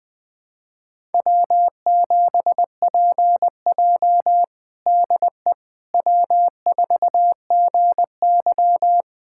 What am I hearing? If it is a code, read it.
Morse code, heard as a keyed tone, about 20 wpm, 700 Hz. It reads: W7PJ DE W4GY
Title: USA Callsign DE Callsign - 4 Characters - 20wpm